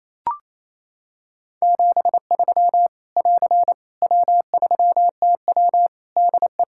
Morse code, heard as a keyed tone, about 28 wpm, 700 Hz. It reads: 73 <AR> W3TW DE
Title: Practice QSO - Exchange 4 - Ending - 28wpm